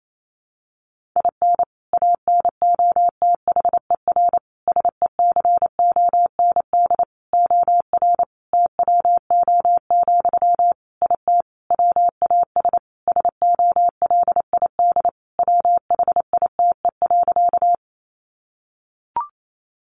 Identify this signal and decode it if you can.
Morse code, heard as a keyed tone, about 28 wpm, 700 Hz. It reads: IN ANOTHER SECOND OR TWO, IT WAS SOLID WHITE.
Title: Sentences from Top 1500 Words 28wpm